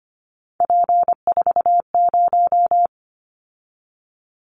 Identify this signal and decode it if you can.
Morse code, heard as a keyed tone, about 25 wpm, 700 Hz. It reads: P40
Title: Callsign Prefixes for Top 100 DXCC - 25wpm